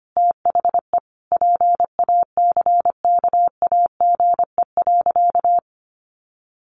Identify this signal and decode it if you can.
Morse code, heard as a keyed tone, about 25 wpm, 700 Hz. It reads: THE PACKAGE.
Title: Additional Easy Phrases 25wpm